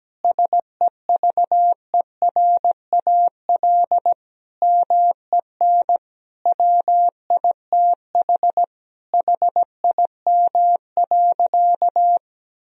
Morse code, heard as a keyed tone, about 17 wpm, 700 Hz. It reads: SEVERAL MEN WITH HIM.
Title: Sentences from Top 1000 Words - Review Part 2 - 17wpm